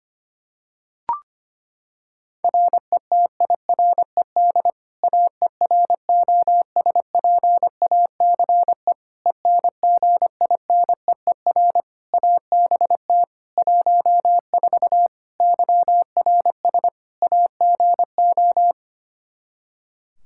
Morse code, heard as a keyed tone, about 25 wpm, 700 Hz. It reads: RETIRED AEROSPACE ENGINEER ABT 14 YRS AGO
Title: K1USN SST QSO Practice 25wpm